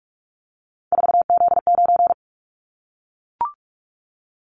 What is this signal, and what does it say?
Morse code, heard as a keyed tone, about 44 wpm, 700 Hz. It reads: 4Z9
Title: Callsign Prefixes for Top 50 DXCC - 45wpm